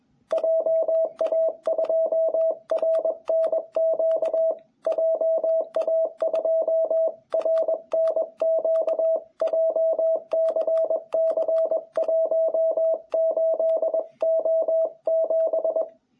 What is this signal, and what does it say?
Morse code, heard as a keyed tone, about 21 wpm, 660 Hz. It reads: JA2RNQ JA2RNQ JCC18O7
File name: jh8cbh / 2021cw